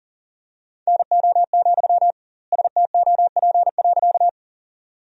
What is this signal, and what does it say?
Morse code, heard as a keyed tone, about 38 wpm, 700 Hz. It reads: NO, STOP.
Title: 2x Word-Spacing - Sentences from Top 300 Words - 40wpm